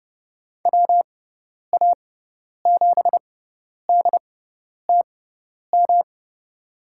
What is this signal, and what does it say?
Morse code, heard as a keyed tone, about 29 wpm, spaced as for 10 wpm, 700 Hz. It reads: WA7DTM
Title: USA Call Signs - 6 Characters - 6x Character Spacing - 30wpm